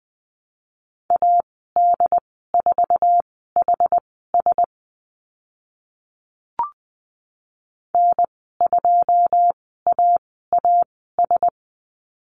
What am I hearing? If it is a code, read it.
Morse code, heard as a keyed tone, about 20 wpm, spaced as for 14 wpm, 700 Hz. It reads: AD4HS N2AAS
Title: 088 Lesson 15g - Number 5 - Call Signs 20wpm